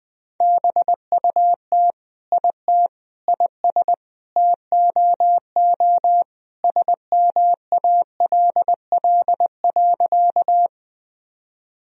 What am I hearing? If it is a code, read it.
Morse code, heard as a keyed tone, about 20 wpm, 700 Hz. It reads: BUT IT IS TOO SMALL.
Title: Sentences from Top 1000 Words - Review Part 1 - 20wpm